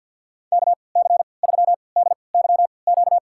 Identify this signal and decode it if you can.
Morse code, heard as a keyed tone, about 48 wpm, spaced as for 28 wpm, 700 Hz. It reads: KC3DYQ
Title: USA Call Signs - 6 Characters - 3x Character Spacing - 50wpm